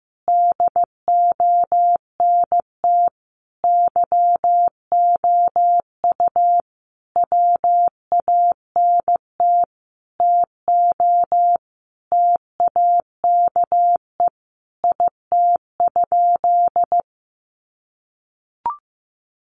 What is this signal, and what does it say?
Morse code, heard as a keyed tone, about 15 wpm, 700 Hz. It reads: DONT YOU WANT TO TAKE IT?
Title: Sentences from Top 300 Words 15wpm